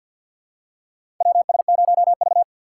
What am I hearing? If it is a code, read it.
Morse code, heard as a keyed tone, about 46 wpm, 700 Hz. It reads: WS0V
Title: USA Call Signs - 4 Characters - 2x Character Spacing - 50wpm